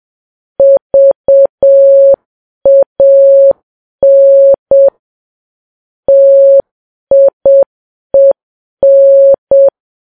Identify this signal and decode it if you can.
Morse code, heard as a keyed tone, about 7 wpm, 550 Hz. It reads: VAN TIEN
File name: Files Web tls / Morse on9fv / morse7kps